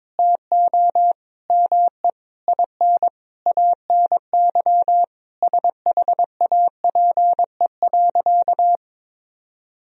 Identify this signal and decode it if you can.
Morse code, heard as a keyed tone, about 22 wpm, 700 Hz. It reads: TO ME IN ANY SHAPE.
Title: Sentences from Top 1000 Words - Review Part 3 - 22wpm